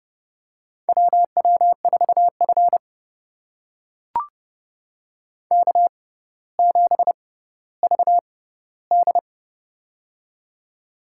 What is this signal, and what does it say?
Morse code, heard as a keyed tone, about 29 wpm, spaced as for 15 wpm, 700 Hz. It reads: WW4F K7VD
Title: USA Call Signs - 4 Characters - 6x Character Spacing - 30wpm